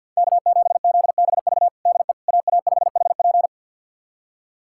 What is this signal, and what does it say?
Morse code, heard as a keyed tone, about 51 wpm, 700 Hz. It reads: K7ZBV DE AA5HP
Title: USA Callsign DE Callsign - 5 Characters - 50wpm